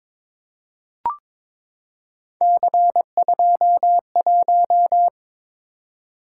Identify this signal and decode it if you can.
Morse code, heard as a keyed tone, about 22 wpm, 700 Hz. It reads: C21
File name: practice / 22wpm / Three Letter-Number Combinations - 22wpm